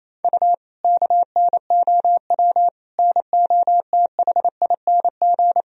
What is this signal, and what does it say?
Morse code, heard as a keyed tone, about 28 wpm, 700 Hz. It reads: U KNOW NOTHING
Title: Sentences from Top 500 Words and Abbreviations 28wpm